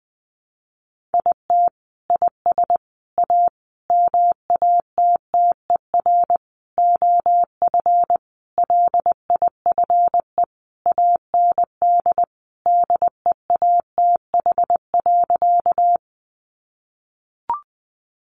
Morse code, heard as a keyed tone, about 20 wpm, 700 Hz. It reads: IT IS A MATTER OF LIFE AND DEATH.
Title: Sentences from Top 1100 Words 20wpm